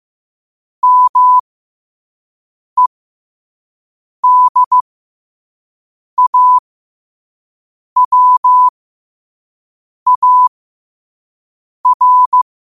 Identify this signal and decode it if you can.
Morse code, heard as a keyed tone, about 15 wpm, spaced as for 5 wpm, 1000 Hz. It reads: MEDAWAR